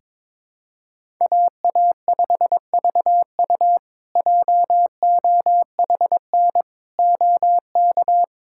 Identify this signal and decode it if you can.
Morse code, heard as a keyed tone, about 22 wpm, 700 Hz. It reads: AA5VU JOHN OK
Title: K1USN SST QSO Practice - 22wpm